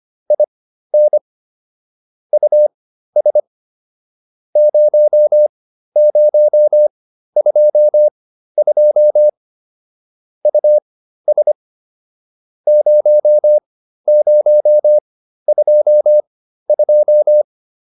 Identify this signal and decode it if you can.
Morse code, heard as a keyed tone, about 25 wpm, spaced as for 13 wpm, 600 Hz. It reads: IN US 0022 US 0022